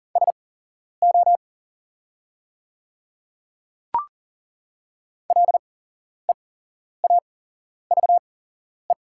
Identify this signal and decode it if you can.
Morse code, heard as a keyed tone, about 40 wpm, spaced as for 11 wpm, 690 Hz. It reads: SO LEAVE
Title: 8x Character Spacing - Top 500 Words - 40wpm